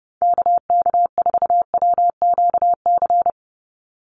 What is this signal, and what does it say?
Morse code, heard as a keyed tone, about 30 wpm, 700 Hz. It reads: KK4WQC
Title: USA Call Signs 30wpm